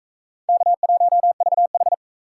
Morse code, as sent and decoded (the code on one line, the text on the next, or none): -.- .---- ...- ....
K1VH